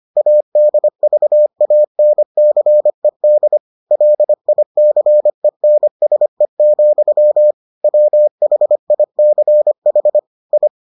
Morse code, heard as a keyed tone, about 25 wpm, 600 Hz. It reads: ADVANCED LICENSE, WHICH I